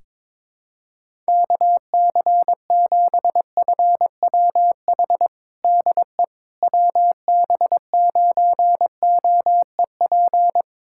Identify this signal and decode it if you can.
Morse code, heard as a keyed tone, about 22 wpm, 700 Hz. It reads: KC7FWH DE WB9OEP
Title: USA Callsign DE Callsign - 6 Characters - 22wpm